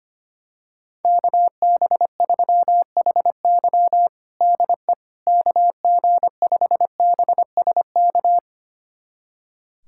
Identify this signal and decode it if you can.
Morse code, heard as a keyed tone, about 25 wpm, 700 Hz. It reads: KB3HY DE KG5BSK
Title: USA Callsign DE Callsign 25wpm